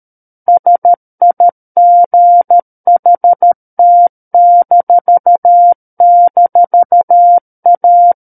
Morse code, heard as a keyed tone, about 13 wpm, 700 Hz. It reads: SIGHT--A
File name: Book One / 04-Chapter 4 - 13 WPM